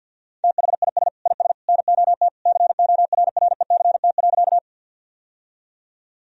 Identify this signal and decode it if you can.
Morse code, heard as a keyed tone, about 52 wpm, 700 Hz. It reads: THIS IS NOT CORRECT.
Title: Sentences from Top 1000 Words - Review Part 2 - 50wpm